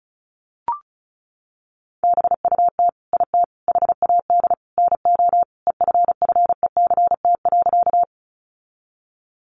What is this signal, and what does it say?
Morse code, heard as a keyed tone, about 34 wpm, 700 Hz. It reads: BUT IT HAD NO EFFECT.
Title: Sentences from Top 1000 Words 35wpm